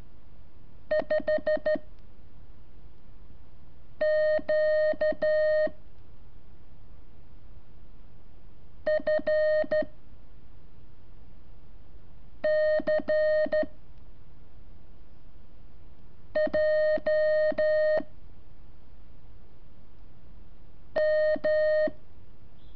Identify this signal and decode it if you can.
Morse code, heard as a keyed tone, about 13 wpm, 620 Hz. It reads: 5 Q F C J M
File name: curso / ejercicio 25